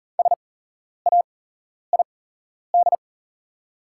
Morse code, heard as a keyed tone, about 40 wpm, spaced as for 11 wpm, 700 Hz. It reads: SAID
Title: 8x Character Spacing - Top 100 Words - 40wpm